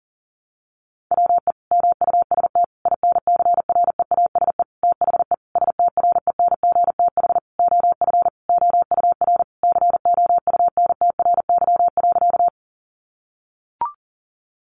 Morse code, heard as a keyed tone, about 40 wpm, 700 Hz. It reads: WE MUST INCREASE THE STRENGTH OF OUR COUNTRY.
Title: Sentences from Top 1300 Words 40wpm